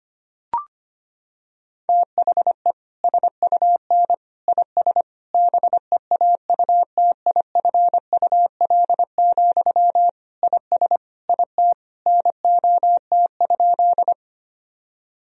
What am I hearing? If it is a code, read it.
Morse code, heard as a keyed tone, about 25 wpm, 700 Hz. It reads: THE SUN IS BEAUTIFUL, IS IT NOT?
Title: Sentences from Top 1000 Words - Review Part 1 - 25wpm